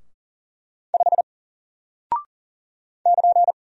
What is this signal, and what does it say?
Morse code, heard as a keyed tone, about 40 wpm, 700 Hz. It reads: 5 <KN>
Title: Single Character (Letter:Number:Punctuation:Prosign) Alt 40wpm